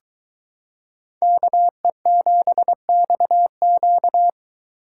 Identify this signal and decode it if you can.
Morse code, heard as a keyed tone, about 22 wpm, 700 Hz. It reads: KE7XQ